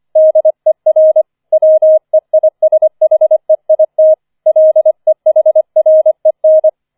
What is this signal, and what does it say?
Morse code, heard as a keyed tone, about 25 wpm, 630 Hz. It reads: DER WEISHEIT LEHREN